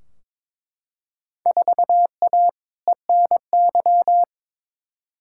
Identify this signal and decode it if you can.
Morse code, heard as a keyed tone, about 21 wpm, 700 Hz. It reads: 4A ENY
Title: Field Day Exchanges Encore - 22wpm